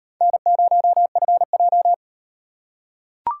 Morse code, heard as a keyed tone, about 38 wpm, 700 Hz. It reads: N0FJ